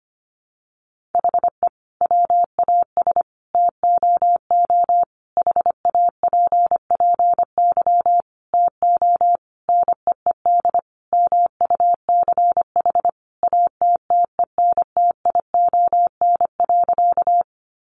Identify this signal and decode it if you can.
Morse code, heard as a keyed tone, about 25 wpm, 700 Hz. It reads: HE WAS TOO HAPPY TO NEED MUCH ATTENTION.
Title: Sentences from Top 1000 Words 25wpm